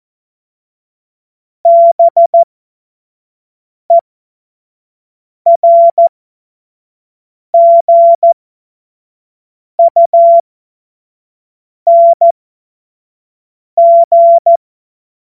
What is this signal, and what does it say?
Morse code, heard as a keyed tone, about 14 wpm, spaced as for 5 wpm, 680 Hz. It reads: BERGUNG
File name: Morsekurs / Witze / Witz 08. November 2022 25 F70